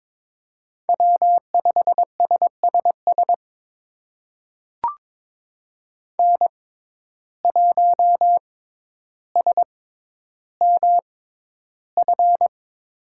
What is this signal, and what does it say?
Morse code, heard as a keyed tone, about 22 wpm, spaced as for 11 wpm, 700 Hz. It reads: W5SSS N1SMF
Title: USA Call Signs - 5 Characters - 6x Character Spacing - 22wpm